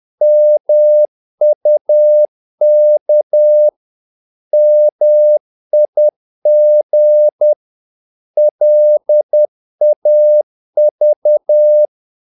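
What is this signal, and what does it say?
Morse code, heard as a keyed tone, about 10 wpm, 600 Hz. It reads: MUK MIG LAV
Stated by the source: Morse practice of short Danish words at 10 wpm, from CW med Gnister.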